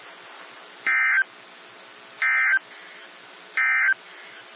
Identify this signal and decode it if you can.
Emergency Alert System (SAME) tones, heard as harsh buzzing bursts. It is heard from 0.7 to 4.0 s.
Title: NOAA Weather Radio - WXJ73